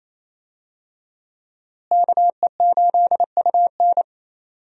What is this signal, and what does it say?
Morse code, heard as a keyed tone, about 28 wpm, 700 Hz. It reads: KE8UN
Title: USA Call Signs - 5 Characters - Repeated 2x - 28wpm